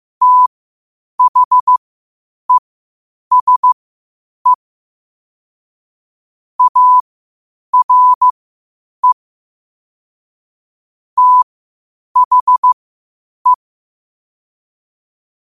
Morse code, heard as a keyed tone, about 15 wpm, spaced as for 8 wpm, 1000 Hz. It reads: THESE ARE THE
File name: qotdfiles / 8wpm